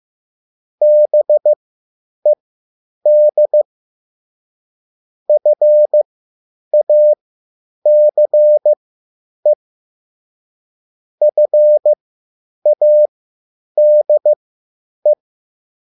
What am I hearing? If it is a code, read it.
Morse code, heard as a keyed tone, about 15 wpm, spaced as for 9 wpm, 600 Hz. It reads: BED FACE FADE
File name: QSL / CW lesson